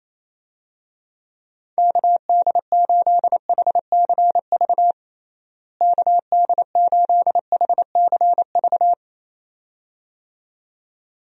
Morse code, heard as a keyed tone, about 28 wpm, 700 Hz. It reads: KD8HCV KD8HCV
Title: USA Call Signs - 6 Characters - Repeated 2x - 28wpm